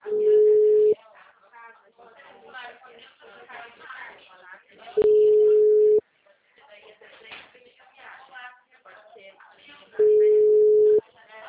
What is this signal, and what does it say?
Ringback tone (Europe, 425 Hz), with speech.